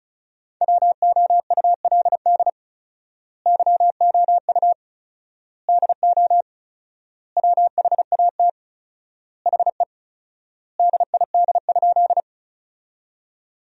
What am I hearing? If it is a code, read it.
Morse code, heard as a keyed tone, about 36 wpm, 700 Hz. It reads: WOULD YOU DO WHAT HE DID?
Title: 4x Word-Spacing - Sentences from Top 100 Words - 35wpm